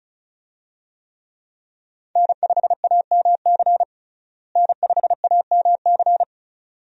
Morse code, heard as a keyed tone, about 36 wpm, 700 Hz. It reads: N5AMC N5AMC
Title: USA Call Signs - 5 Characters - Repeated 2x - 35wpm